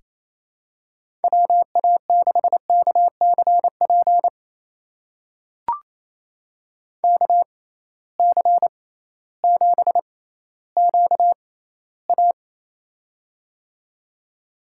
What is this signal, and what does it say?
Morse code, heard as a keyed tone, about 27 wpm, spaced as for 15 wpm, 700 Hz. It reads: WA6KCP KC7QA
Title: USA Call Signs - 6x Character Spacing - 28wpm